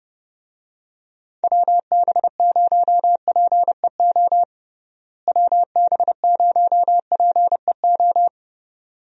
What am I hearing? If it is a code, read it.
Morse code, heard as a keyed tone, about 32 wpm, 700 Hz. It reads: WB0PEO WB0PEO
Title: USA Call Signs - 6 Characters - Repeated 2x - 30wpm